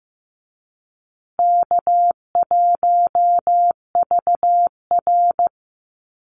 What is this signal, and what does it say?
Morse code, heard as a keyed tone, about 14 wpm, 700 Hz. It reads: K1VR
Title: Easy Call Signs Repeated 2x - 15wpm